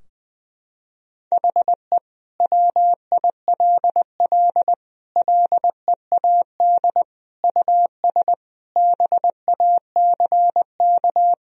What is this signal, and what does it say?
Morse code, heard as a keyed tone, about 20 wpm, 700 Hz. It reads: HE WILL LEAD US BACK
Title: Sentences from Top 1000 Words - Review Part 2 - 20wpm